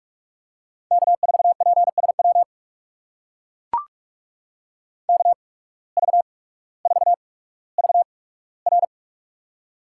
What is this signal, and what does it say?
Morse code, heard as a keyed tone, about 44 wpm, spaced as for 19 wpm, 700 Hz. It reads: K4PSW KV4VR
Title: USA Call Signs - 5 Characters - 8x Character Spacing - 45wpm